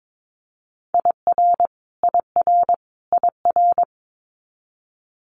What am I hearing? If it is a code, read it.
Morse code, heard as a keyed tone, about 22 wpm, 700 Hz. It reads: IR IR IR
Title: Multi-letter Phonemes Repeated 3x 22wpm